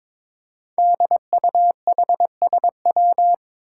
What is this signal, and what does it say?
Morse code, heard as a keyed tone, about 22 wpm, 700 Hz. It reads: DUHSW
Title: Five Challenging Letter-Number Combinations 22wpm